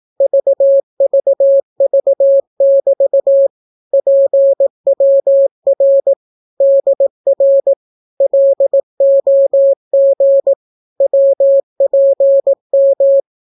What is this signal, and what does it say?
Morse code, heard as a keyed tone, about 18 wpm, 550 Hz. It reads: VVV<BT> PWR DR LOG WPM
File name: naukapodstaw / skroty / skroty